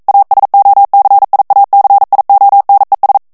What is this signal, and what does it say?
Morse code, heard as a keyed tone, about 42 wpm, 800 Hz. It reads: ASOCIACIONES
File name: morse mp3 / ESPAGNOL TEXTE EN CLAIR 1 VITESSE 40 MOTS MINUTES